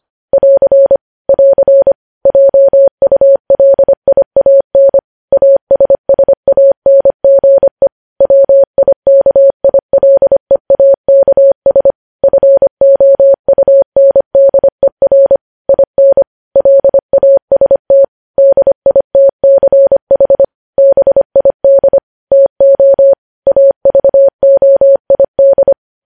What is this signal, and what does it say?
Morse code, heard as a keyed tone, about 25 wpm, 550 Hz. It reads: <AR> <AR> JULIAN ASSANGE WIKILEAKS FOUNDER IN LAST DITCH BID TO AVOID
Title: News Headlines at 25